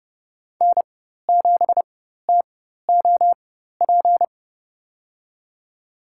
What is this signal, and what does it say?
Morse code, heard as a keyed tone, about 30 wpm, spaced as for 14 wpm, 700 Hz. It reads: N7TOP
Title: USA Call Signs - 5 Characters - 4x Character Spacing - 30wpm